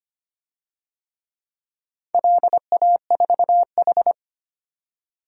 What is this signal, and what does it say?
Morse code, heard as a keyed tone, about 25 wpm, 700 Hz. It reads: LA4H
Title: DXCC-100 Call Signs - 4 Characters - 25wpm